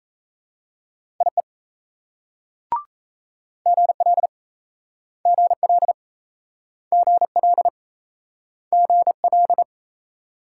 Morse code, heard as a keyed tone, about 38 wpm, 700 Hz. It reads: IE GL GL GL GL